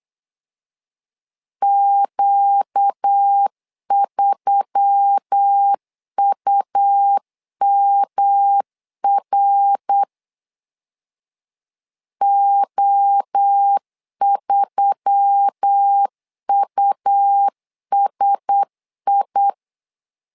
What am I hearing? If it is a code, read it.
Morse code, heard as a keyed tone, about 8 wpm, 800 Hz. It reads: Q3UMR O3USI
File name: LEZIONI / 4 Q1S3 8wpm 3037